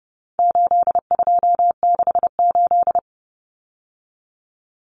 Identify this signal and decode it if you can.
Morse code, heard as a keyed tone, about 30 wpm, 700 Hz. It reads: 8268
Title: Numbers - 4-Digit 30wpm